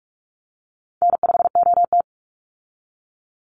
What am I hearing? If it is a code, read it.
Morse code, heard as a keyed tone, about 46 wpm, 700 Hz. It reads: N5OT
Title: Easy Call Signs - 45wpm